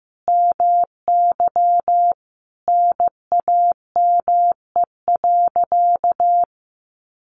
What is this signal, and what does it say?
Morse code, heard as a keyed tone, about 15 wpm, 700 Hz. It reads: MY NAME.
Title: Sentences from Top 500 Words 15wpm